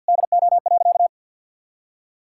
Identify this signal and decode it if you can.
Morse code, heard as a keyed tone, about 50 wpm, 700 Hz. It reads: DO.
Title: Sentences from Top 1000 Words - Review Part 5 - 50wpm